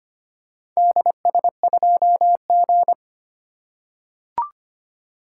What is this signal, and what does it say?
Morse code, heard as a keyed tone, about 25 wpm, 700 Hz. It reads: DS2G
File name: practice / 25wpm / Four Letter-Number Combinations - 25wpm